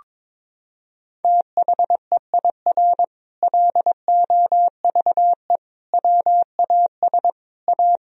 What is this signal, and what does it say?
Morse code, heard as a keyed tone, about 22 wpm, 700 Hz. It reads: THEIR LOVE WAS A
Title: Sentences from Top 1000 Words - Review Part 4 - 22wpm